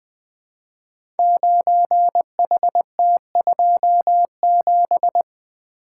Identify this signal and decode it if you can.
Morse code, heard as a keyed tone, about 20 wpm, 700 Hz. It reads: 9HT27
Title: Code Groups - 5 Letter:Numbers - 20wpm